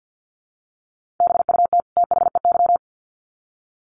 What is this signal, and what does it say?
Morse code, heard as a keyed tone, about 50 wpm, 700 Hz. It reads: BUT THEY
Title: Sets of 2 Words 50wpm